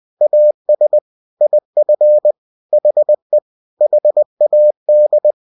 Morse code, heard as a keyed tone, about 20 wpm, 600 Hz. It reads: AS IF HE HAD